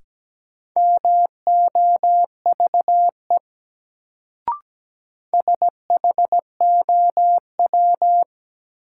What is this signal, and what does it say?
Morse code, heard as a keyed tone, about 17 wpm, 700 Hz. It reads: MOVE SHOW
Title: Sentences from Top 200 Words - New Word Drill - 17wpm